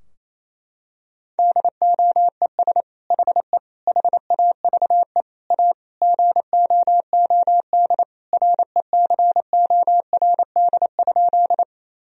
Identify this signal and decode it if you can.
Morse code, heard as a keyed tone, about 28 wpm, 700 Hz. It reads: DOES HE HAVE A GOOD RECORD?
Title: Sentences from Top 1000 Words - Review Part 3 - 28wpm